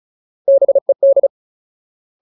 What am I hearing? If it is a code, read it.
Morse code, heard as a keyed tone, about 34 wpm, 550 Hz. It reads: BED
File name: Ultimate Word List / 35WPM Ultimate